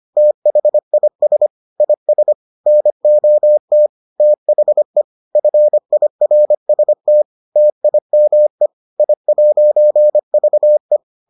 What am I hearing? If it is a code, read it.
Morse code, heard as a keyed tone, about 25 wpm, 600 Hz. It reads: THIS IS NOT THE FIRST TIME I'VE